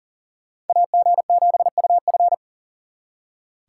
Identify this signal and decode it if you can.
Morse code, heard as a keyed tone, about 52 wpm, 700 Hz. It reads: AG7UF